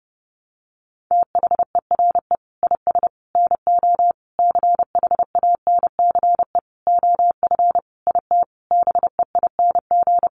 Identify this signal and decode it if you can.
Morse code, heard as a keyed tone, about 28 wpm, 700 Hz. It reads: THERE IS NO CHANCE OF IT BEING
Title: Sentences from Top 800 Words 30wpm